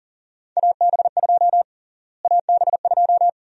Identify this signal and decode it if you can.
Morse code, heard as a keyed tone, about 40 wpm, 700 Hz. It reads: AB2 AB2
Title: USA Call Sign Prefixes - Repeated 2x - 40wpm